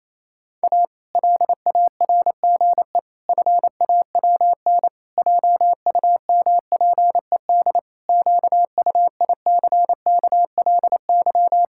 Morse code, heard as a keyed tone, about 28 wpm, 700 Hz. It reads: A LARGE FAWN JUMPED QUICKLY
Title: Pangram Challenge v2 - 28wpm